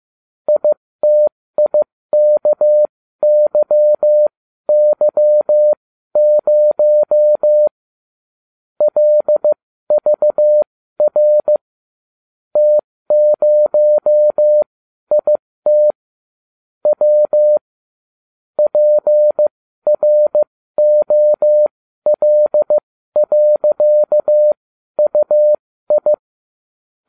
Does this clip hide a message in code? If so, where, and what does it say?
Yes: Morse code from 0.5 to 26.1 s, heard as a keyed tone, about 15 wpm, spaced as for 12 wpm, 600 Hz. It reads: ITIKYY0 LVR T0IT W PROL.UI